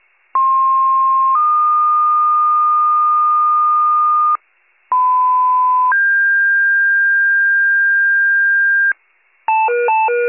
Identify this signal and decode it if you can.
Two-tone page, heard as a one-second tone, then a three-second tone, about 1060 then 1220 Hz, about 1010 then 1600 Hz, with noise. It is heard from 0.3 to 8.9 s.